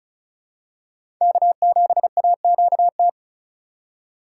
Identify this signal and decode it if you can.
Morse code, heard as a keyed tone, about 34 wpm, 700 Hz. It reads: K7AQT
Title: USA Call Signs - 5 Characters - 2x Character Spacing - 35wpm